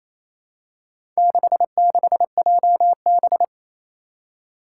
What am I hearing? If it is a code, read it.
Morse code, heard as a keyed tone, about 28 wpm, 700 Hz. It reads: 66JB
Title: Four Challenging Letter-Number Combinations 28wpm